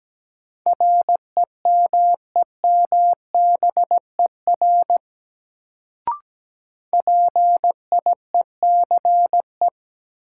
Morse code, heard as a keyed tone, about 17 wpm, 700 Hz. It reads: REMEMBER PIECE
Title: Sentences from Top 400 Words - New Word Drill - 17wpm